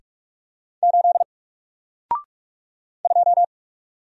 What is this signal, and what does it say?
Morse code, heard as a keyed tone, about 44 wpm, 700 Hz. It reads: 8 2
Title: Single Letter-Number Alt 45wpm